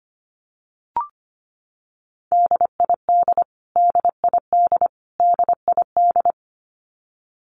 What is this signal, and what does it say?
Morse code, heard as a keyed tone, about 25 wpm, 700 Hz. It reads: DID DID DID
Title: Top 500 Words Repeated 25wpm